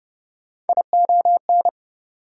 Morse code, heard as a keyed tone, about 41 wpm, spaced as for 36 wpm, 690 Hz. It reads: ION
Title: Speed-Racing Prefixes and Suffixes 30wpm